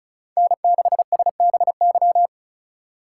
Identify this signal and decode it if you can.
Morse code, heard as a keyed tone, about 34 wpm, 700 Hz. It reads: N6SBY
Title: USA Call Signs - 5 Characters - 4x Character Spacing - 35wpm